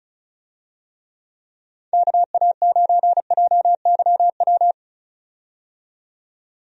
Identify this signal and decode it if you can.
Morse code, heard as a keyed tone, about 35 wpm, 700 Hz. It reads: KA9JYW
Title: USA Call Signs - 6 Characters - 35wpm